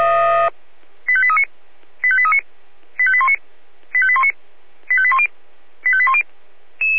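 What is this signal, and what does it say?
Five-tone selective call (ZVEI), heard as quick runs of five tones, tones of about 70 ms. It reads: 85839 85839 85819 85819 85810 85810
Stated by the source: ZVEI five-tone selective call received by SDR.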